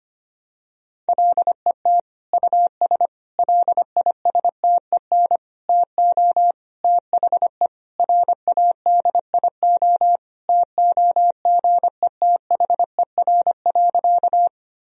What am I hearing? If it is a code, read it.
Morse code, heard as a keyed tone, about 25 wpm, 700 Hz. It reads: LET US LISTEN TO THE RADIO TOGETHER.
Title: Sentences from Top 1000 Words - Review Part 3 - 25wpm